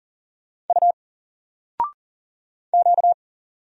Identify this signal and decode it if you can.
Morse code, heard as a keyed tone, about 40 wpm, 700 Hz. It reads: U Q
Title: Single Letter-Number Alt 40wpm